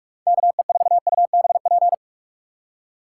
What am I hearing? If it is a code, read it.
Morse code, heard as a keyed tone, about 46 wpm, 700 Hz. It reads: KE4UBP